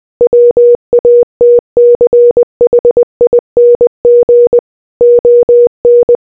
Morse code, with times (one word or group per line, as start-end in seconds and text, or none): 0.2-4.6 WATCHING
5.0-6.2 ON